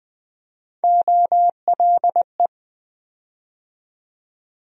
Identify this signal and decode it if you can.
Morse code, heard as a keyed tone, about 20 wpm, 700 Hz. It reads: OLE
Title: USA Call Sign Postfixes - 3 Letters 20wpm